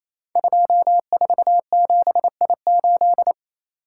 Morse code, heard as a keyed tone, about 27 wpm, 700 Hz. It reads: 247I8
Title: Code Groups - 5 Letter:Numbers - 28wpm